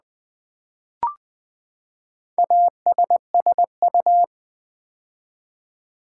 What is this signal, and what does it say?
Morse code, heard as a keyed tone, about 20 wpm, 700 Hz. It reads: ASSU
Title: Build A Word - 4 Letters At A Time - 10-Letter Words - 20wpm